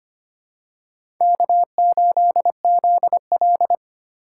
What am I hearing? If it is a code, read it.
Morse code, heard as a keyed tone, about 25 wpm, 700 Hz. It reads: K8ZL
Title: USA Call Signs - 4x Character Spacing - 25wpm